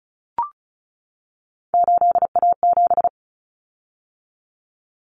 Morse code, heard as a keyed tone, about 34 wpm, 700 Hz. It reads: 8A7
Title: Common Cut Numbers - 3-Digit 35wpm